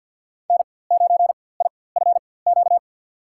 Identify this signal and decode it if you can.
Morse code, heard as a keyed tone, about 51 wpm, spaced as for 23 wpm, 700 Hz. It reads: N9IFQ
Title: USA Call Signs - 5 Characters - 4x Character Spacing - 50wpm